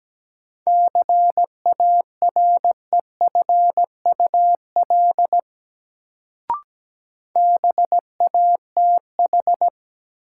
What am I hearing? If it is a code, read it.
Morse code, heard as a keyed tone, about 17 wpm, 700 Hz. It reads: CAREFUL BATH
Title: Sentences from Top 1600 Words - New Word Drill - 17wpm